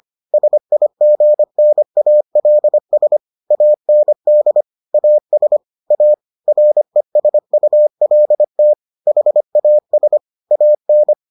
Morse code, heard as a keyed tone, about 25 wpm, 600 Hz. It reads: SIGNALS AND AS A RESULT HAS AN